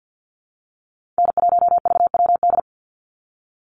Morse code, heard as a keyed tone, about 49 wpm, 700 Hz. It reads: N1VWD
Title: NA SOTA Chaser Callsigns 50wpm